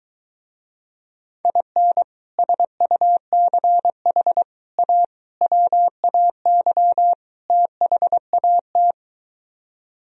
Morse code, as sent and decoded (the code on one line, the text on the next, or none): .. -. / ... ..- -.-. .... / .- / .-- .- -.-- / - .... .- -
IN SUCH A WAY THAT